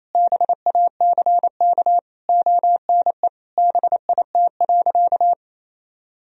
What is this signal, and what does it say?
Morse code, heard as a keyed tone, about 28 wpm, 700 Hz. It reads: BACK ONE BIT.
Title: Sentences from Top 1000 Words - Review Part 1 - 28wpm